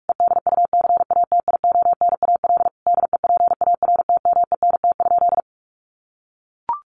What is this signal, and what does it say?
Morse code, heard as a keyed tone, about 44 wpm, 700 Hz. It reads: EDUCATIONAL DEPARTMENT?
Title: Sentences from Top 1200 Words 45wpm